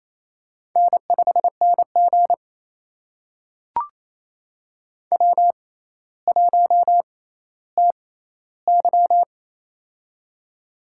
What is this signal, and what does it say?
Morse code, heard as a keyed tone, about 28 wpm, spaced as for 14 wpm, 700 Hz. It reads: N5NG W1TY
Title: USA Call Signs - 4 Characters - 6x Character Spacing - 28wpm